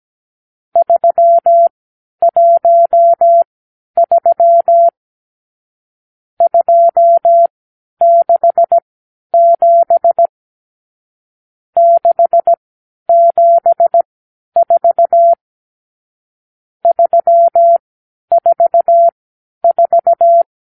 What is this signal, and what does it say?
Morse code, heard as a keyed tone, about 17 wpm, spaced as for 10 wpm, 680 Hz. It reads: 313 267 674 344
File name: k7qo / Lesson 54 (9)